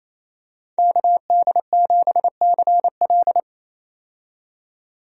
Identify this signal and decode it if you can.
Morse code, heard as a keyed tone, about 28 wpm, 700 Hz. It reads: KD7CL
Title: USA Call Signs - 5 Characters - 28wpm